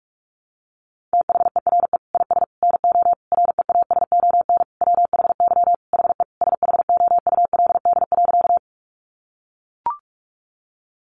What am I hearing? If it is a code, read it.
Morse code, heard as a keyed tone, about 44 wpm, 700 Hz. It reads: THERE IS NO REASON WHY HE SHOULD.
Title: Sentences from Top 700 Words 45wpm